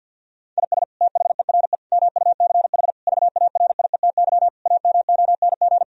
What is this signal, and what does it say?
Morse code, heard as a keyed tone, about 50 wpm, 700 Hz. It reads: IS THERE MUCH VARIETY AMONG
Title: Sentences from Top 1000 Words - Review Part 4 - 50wpm